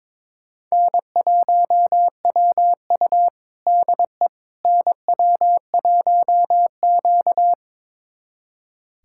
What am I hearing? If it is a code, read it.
Morse code, heard as a keyed tone, about 22 wpm, 700 Hz. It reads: N1WU DE NW1Q
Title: USA Callsign DE Callsign - 4 Characters - 22wpm